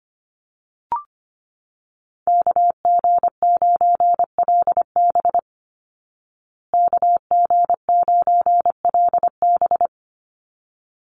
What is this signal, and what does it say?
Morse code, heard as a keyed tone, about 25 wpm, 700 Hz. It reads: KG9LB KG9LB
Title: Call Signs Repeated 3x - 25wpm